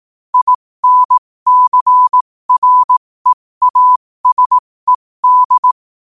Morse code, heard as a keyed tone, about 18 wpm, spaced as for 15 wpm, 1000 Hz. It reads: INCREASED
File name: qotdfiles / 15wpm